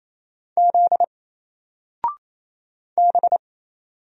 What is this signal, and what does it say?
Morse code, heard as a keyed tone, about 27 wpm, 700 Hz. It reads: Z B
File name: Single-Letter-Number-28wpm